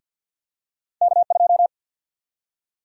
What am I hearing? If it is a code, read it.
Morse code, heard as a keyed tone, about 51 wpm, 700 Hz. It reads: K2
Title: USA Call Sign Prefixes - Repeated 2x - 50wpm